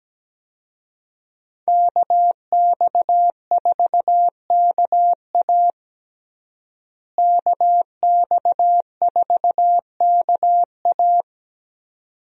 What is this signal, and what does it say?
Morse code, heard as a keyed tone, about 17 wpm, 700 Hz. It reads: KX4KA KX4KA
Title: USA Call Signs - 5 Characters - Repeated 2x - 17wpm